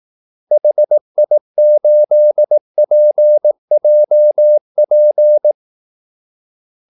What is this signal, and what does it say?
Morse code, heard as a keyed tone, about 18 wpm, 600 Hz. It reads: HI8PJP
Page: Kaldesignaler 18 wpm